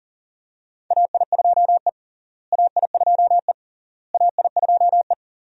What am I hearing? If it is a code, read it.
Morse code, heard as a keyed tone, about 40 wpm, 700 Hz. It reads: AI2E AI2E AI2E
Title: USA Call Signs - 4 Characters - Repeated 3x - 40wpm